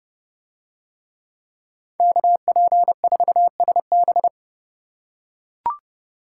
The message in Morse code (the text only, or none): KP4SB